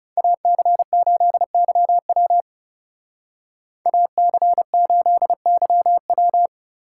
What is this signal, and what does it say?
Morse code, heard as a keyed tone, about 32 wpm, 700 Hz. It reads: AC8YW AC8YW
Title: Speed Racing - USA Call Signs 30wpm